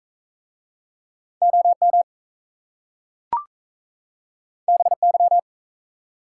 Morse code, heard as a keyed tone, about 38 wpm, 700 Hz. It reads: OM BY